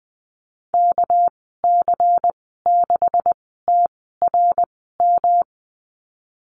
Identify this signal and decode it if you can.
Morse code, heard as a keyed tone, about 20 wpm, spaced as for 15 wpm, 700 Hz. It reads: KC6TRM
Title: 238 Lesson 38g - Letter q - Call Signs 20wpm